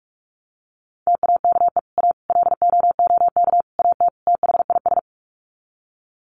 Morse code, heard as a keyed tone, about 44 wpm, 700 Hz. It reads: TAKE A LOOK AT THIS
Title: Sets of 5 Words 45wpm